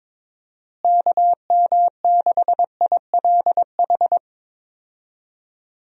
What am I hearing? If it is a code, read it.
Morse code, heard as a keyed tone, about 22 wpm, 700 Hz. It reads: KM6ILH
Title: USA Call Signs - 6 Characters - Repeated 2x - 22wpm